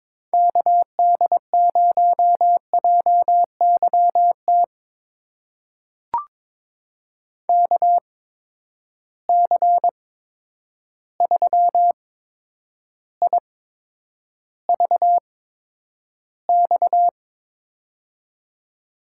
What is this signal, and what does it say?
Morse code, heard as a keyed tone, about 22 wpm, spaced as for 9 wpm, 700 Hz. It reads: KD0JYT KC3IVX
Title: USA Call Signs - 6 Characters - 8x Character Spacing - 22wpm